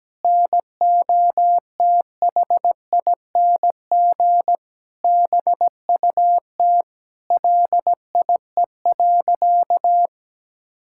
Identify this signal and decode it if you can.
Morse code, heard as a keyed tone, about 17 wpm, 700 Hz. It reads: NOTHING BUT LIE.
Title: Sentences from Top 1600 Words 17wpm